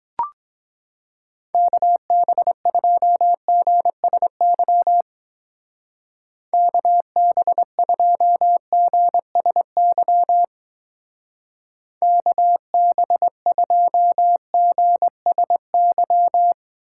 Morse code, heard as a keyed tone, about 21 wpm, 700 Hz. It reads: KB2GSY KB2GSY KB2GSY
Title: Speed Racing - USA 2x3 Call Signs 17wpm